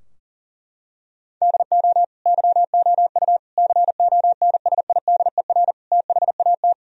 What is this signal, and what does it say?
Morse code, heard as a keyed tone, about 37 wpm, 700 Hz. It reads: DO YOU CONSIDER THAT
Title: Sentences from Top 1000 Words - Review Part 4 - 40wpm